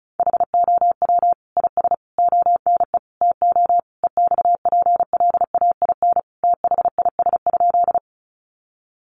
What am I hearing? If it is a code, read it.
Morse code, heard as a keyed tone, about 34 wpm, 700 Hz. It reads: HOW IS ONE TO EXPLAIN THIS?
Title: Sentences from Top 1100 Words 35wpm